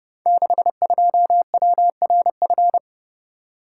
Morse code, heard as a keyed tone, about 30 wpm, 700 Hz. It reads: 62WRF
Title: Code Groups - 5 Letter:Numbers - 30wpm